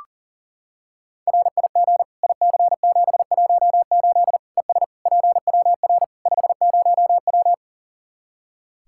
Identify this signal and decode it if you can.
Morse code, heard as a keyed tone, about 40 wpm, 700 Hz. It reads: RIG IC718 ES PWR 50W
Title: Practice QSO - Exchange 2 - Info - 40wpm